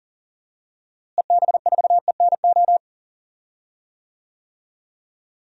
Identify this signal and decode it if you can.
Morse code, heard as a keyed tone, about 40 wpm, 700 Hz. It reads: EB4ENO
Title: DXCC-100 Call Signs - 6 Characters - Repeated 2x - 40wpm